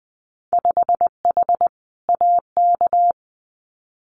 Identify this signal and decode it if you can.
Morse code, heard as a keyed tone, about 19 wpm, 700 Hz. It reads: 5H AK
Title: Winter Field Day Exchanges Encore 20wpm